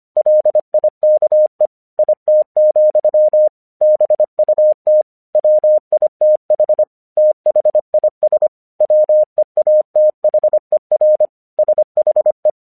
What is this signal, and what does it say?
Morse code, heard as a keyed tone, about 25 wpm, 610 Hz. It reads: LIKE IT, BUT WITH THIS WEATHER SHE
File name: 1 English 25wpm / AQC17